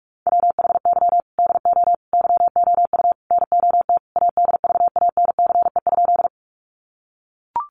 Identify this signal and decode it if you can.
Morse code, heard as a keyed tone, about 44 wpm, 700 Hz. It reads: WHY DO YOU NOT ADVANCE?
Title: Sentences from Top 1400 Words 45wpm